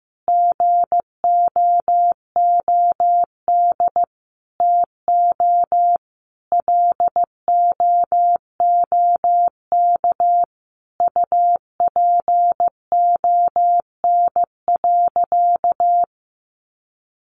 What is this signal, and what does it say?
Morse code, heard as a keyed tone, about 15 wpm, 700 Hz. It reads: GOOD TO LOOK UPON.
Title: Sentences from Top 1400 Words 15wpm